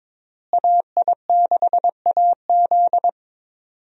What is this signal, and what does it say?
Morse code, heard as a keyed tone, about 22 wpm, 700 Hz. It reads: AI6AZ
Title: USA Call Signs - 5 Characters - Repeated 2x - 22wpm